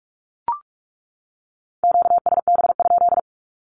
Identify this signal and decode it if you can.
Morse code, heard as a keyed tone, about 44 wpm, 700 Hz. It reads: QSB?
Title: Q-Codes 45wpm